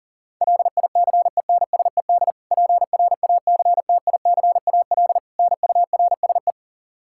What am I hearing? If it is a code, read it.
Morse code, heard as a keyed tone, about 40 wpm, 700 Hz. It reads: LICENSED PRACTICAL NURSE
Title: Practice QSO - Exchange 3 - More Info - 40wpm